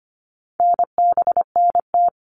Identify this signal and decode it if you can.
Morse code, heard as a keyed tone, about 25 wpm, 700 Hz. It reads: NBNT
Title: All Cut Numbers - 4-Digit 25wpm